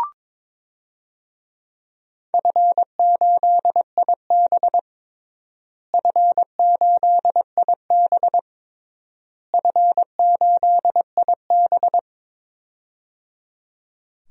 Morse code, heard as a keyed tone, about 22 wpm, 700 Hz. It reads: F8IB F8IB F8IB
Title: DXCC-100 Call Signs - 4 Characters - Repeated 3x - 22wpm